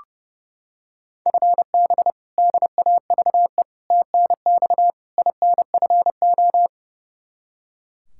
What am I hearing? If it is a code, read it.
Morse code, heard as a keyed tone, about 30 wpm, 700 Hz. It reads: FB DAVE TNX INFO
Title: K1USN SST QSO Practice 30wpm